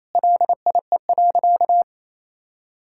Morse code, heard as a keyed tone, about 28 wpm, 700 Hz. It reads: LIE.
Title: Sentences from Top 1600 Words 28wpm